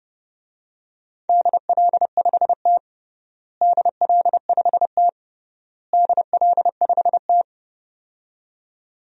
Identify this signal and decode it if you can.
Morse code, heard as a keyed tone, about 30 wpm, 700 Hz. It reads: DL5T DL5T DL5T
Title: DXCC-100 Call Signs - 4 Characters - Repeated 3x - 30wpm